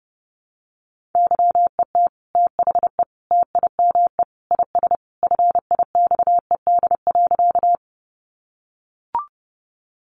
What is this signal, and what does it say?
Morse code, heard as a keyed tone, about 30 wpm, 700 Hz. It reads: YET THE TIME IS FIXED.
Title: Sentences from Top 1100 Words 30wpm